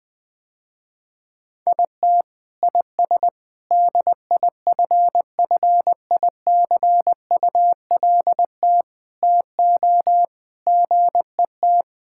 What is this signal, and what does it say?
Morse code, heard as a keyed tone, about 20 wpm, 700 Hz. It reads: IT IS DIFFICULT TO GET
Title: Sentences with 9-Word Phrases - Spoken Beginning - 20wpm